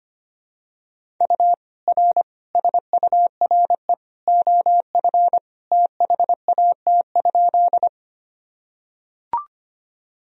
Morse code, heard as a keyed tone, about 25 wpm, 700 Hz. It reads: U R SURE OF THAT?
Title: Sentences from Top 300 Words and Abbreviations 25wpm